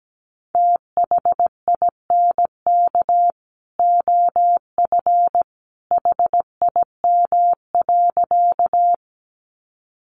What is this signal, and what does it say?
Morse code, heard as a keyed tone, about 17 wpm, 700 Hz. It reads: THINK OF HIM.
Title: Sentences from Top 200 Words 17wpm